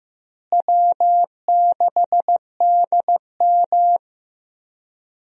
Morse code, heard as a keyed tone, about 15 wpm, 700 Hz. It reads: W6DM
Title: USA Call Signs - 4 Characters - Repeated 3x - 15wpm